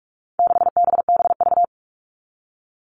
Morse code, heard as a keyed tone, about 44 wpm, 700 Hz. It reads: 6BBV
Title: All Cut Numbers - 4-Digit 45wpm